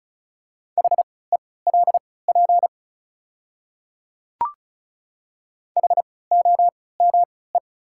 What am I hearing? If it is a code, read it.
Morse code, heard as a keyed tone, about 36 wpm, spaced as for 20 wpm, 690 Hz. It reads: HELP HOME
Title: 3x Character Spacing - Top 300 Words - 35wpm